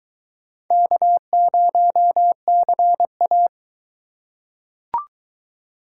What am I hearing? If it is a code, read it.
Morse code, heard as a keyed tone, about 20 wpm, 700 Hz. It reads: K0CA